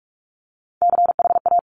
Morse code, heard as a keyed tone, about 46 wpm, 700 Hz. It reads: CHA
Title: Top Three Letter Combinations in Words 45wpm